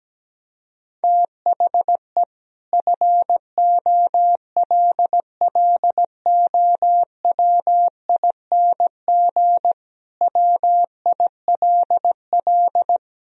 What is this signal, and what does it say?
Morse code, heard as a keyed tone, about 17 wpm, 700 Hz. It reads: THE FOLLOWING WILL
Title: Sentences from Top 1000 Words - Review Part 4 - 17wpm